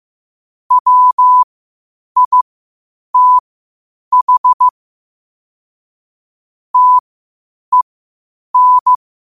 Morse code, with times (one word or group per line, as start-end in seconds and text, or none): 0.7-4.7 WITH
6.7-8.9 TEN